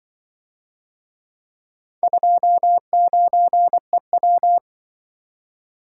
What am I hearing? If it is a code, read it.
Morse code, heard as a keyed tone, about 24 wpm, 700 Hz. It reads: 29EW
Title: Speed-Racing - Four Letter-Number Combinations - 20wpm